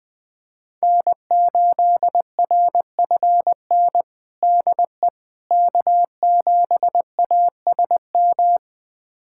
Morse code, heard as a keyed tone, about 20 wpm, 700 Hz. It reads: N8RFN DE K7ASM
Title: USA Callsign DE Callsign - 5 Characters - 20wpm